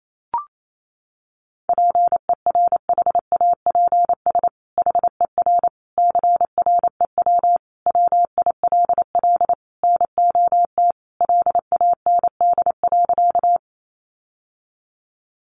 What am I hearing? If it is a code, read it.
Morse code, heard as a keyed tone, about 28 wpm, 700 Hz. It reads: PERHAPS HER CREW WILL NOT LAND.
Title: Sentences from Top 1500 Words 28wpm